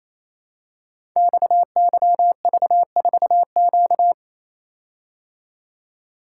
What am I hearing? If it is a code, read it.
Morse code, heard as a keyed tone, about 28 wpm, 700 Hz. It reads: XYV4Q
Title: Code Groups - 5 Letter:Numbers - 28wpm